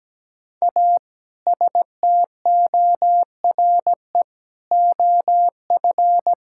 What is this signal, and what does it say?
Morse code, heard as a keyed tone, about 17 wpm, 700 Hz. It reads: A STORE OF
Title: Sentences from Top 1000 Words - Review Part 5 - 17wpm